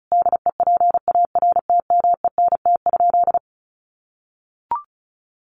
Morse code, heard as a keyed tone, about 34 wpm, 700 Hz. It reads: DEPARTMENT?
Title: Sentences from Top 1200 Words 35wpm